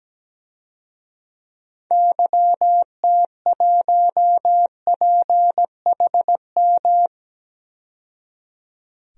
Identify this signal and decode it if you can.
Morse code, heard as a keyed tone, about 17 wpm, 700 Hz. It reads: YT1PHM
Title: DXCC-100 Call Signs - 6 Characters - 17wpm